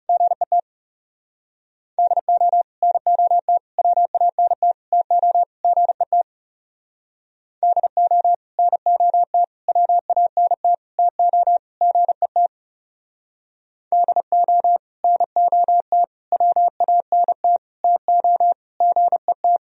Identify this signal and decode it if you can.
Morse code, heard as a keyed tone, about 34 wpm, 700 Hz. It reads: GET DO NOT WANT TO GET DO NOT WANT TO GET DO NOT WANT TO GET
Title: Speed-Racing - Sets of 5 Words 30wpm